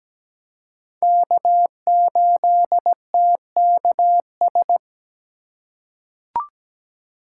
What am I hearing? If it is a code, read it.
Morse code, heard as a keyed tone, about 17 wpm, 700 Hz. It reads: K8TKS
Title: USA Call Signs - 5 Characters - 4x Character Spacing - 17wpm